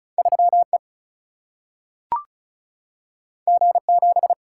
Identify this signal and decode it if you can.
Morse code, heard as a keyed tone, about 36 wpm, 700 Hz. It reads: 3 E G 7
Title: Single-Letter Single-Number 35wpm